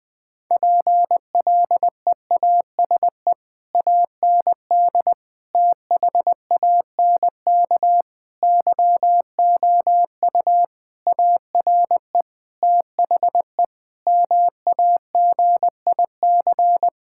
Morse code, heard as a keyed tone, about 20 wpm, 700 Hz. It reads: PLEASE AND THANK YOU ARE THE MAGIC
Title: Difficult Sentences with Hints - 20wpm